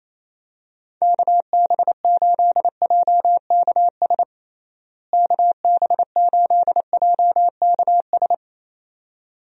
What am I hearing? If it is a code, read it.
Morse code, heard as a keyed tone, about 28 wpm, 700 Hz. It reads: KB8JKS KB8JKS
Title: USA Call Signs - 6 Characters - Repeated 2x - 28wpm